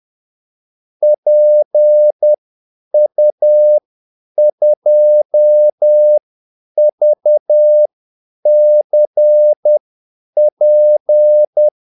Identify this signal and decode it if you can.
Morse code, heard as a keyed tone, about 10 wpm, spaced as for 8 wpm, 600 Hz. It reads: PU2VCP
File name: Callsigns-10-8wpm